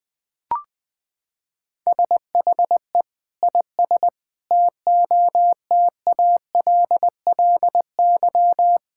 Morse code, heard as a keyed tone, about 20 wpm, 700 Hz. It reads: SHE IS TOTALLY
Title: Sentences from Top 1600 Words 20wpm